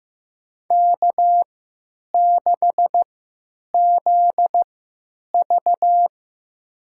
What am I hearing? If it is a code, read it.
Morse code, heard as a keyed tone, about 15 wpm, spaced as for 9 wpm, 700 Hz. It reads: K6ZV
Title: USA Call Signs - 3x Character Spacing - 15wpm